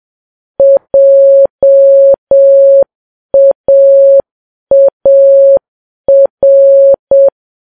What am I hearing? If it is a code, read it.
Morse code, heard as a keyed tone, about 7 wpm, 550 Hz. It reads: JAAR
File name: Files Web tls / Morse on9fv / morse7kps